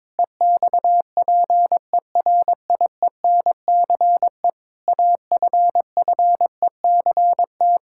Morse code, heard as a keyed tone, about 22 wpm, 700 Hz. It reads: EXPERIENCE AFFECT
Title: Sentences from Top 1000 Words - Review Part 6 - 22wpm